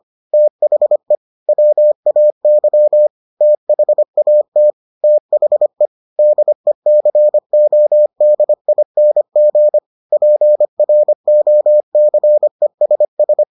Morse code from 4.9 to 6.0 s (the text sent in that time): THE